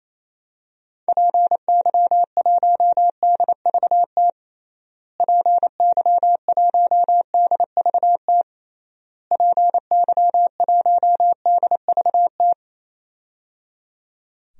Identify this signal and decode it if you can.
Morse code, heard as a keyed tone, about 28 wpm, 700 Hz. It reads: PY1DVT PY1DVT PY1DVT
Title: DXCC-100 Call Signs - 6 Characters - Repeated 3x - 28wpm